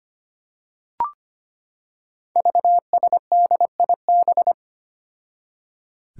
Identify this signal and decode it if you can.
Morse code, heard as a keyed tone, about 25 wpm, 700 Hz. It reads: VSDIB
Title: Five Challenging Letter-Number Combinations 25wpm